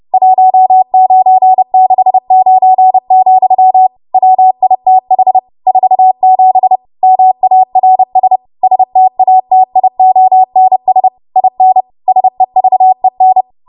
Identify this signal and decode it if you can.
Morse code, heard as a keyed tone, about 30 wpm, 750 Hz. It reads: 1969, WITH 47 MARS STATIONS IN SEVEN
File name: Archive / 30 WPM / 30WPM